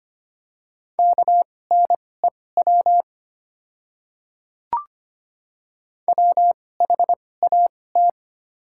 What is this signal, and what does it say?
Morse code, heard as a keyed tone, about 25 wpm, spaced as for 18 wpm, 700 Hz. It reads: KNEW WHAT
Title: 2x Character Spacing - Top 500 Words - 25wpm